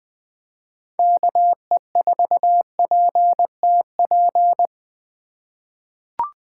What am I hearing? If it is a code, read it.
Morse code, heard as a keyed tone, about 20 wpm, 700 Hz. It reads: KE4PTP
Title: 2023 POTA Hunter Call Signs - 20wpm